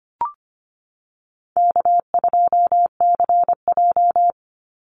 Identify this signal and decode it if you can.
Morse code, heard as a keyed tone, about 25 wpm, 700 Hz. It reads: K2CJ
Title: Easy Call Signs Repeated 2x - 25wpm